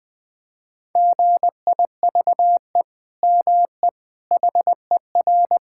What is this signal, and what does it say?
Morse code, heard as a keyed tone, about 21 wpm, 700 Hz. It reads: GIVE ME HER
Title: Sentences from Top 1000 Words - Review Part 6 - 20wpm